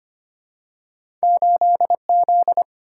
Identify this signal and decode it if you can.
Morse code, heard as a keyed tone, about 25 wpm, 700 Hz. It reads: 8Z
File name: Two Letter-Number Combinations - 25wpm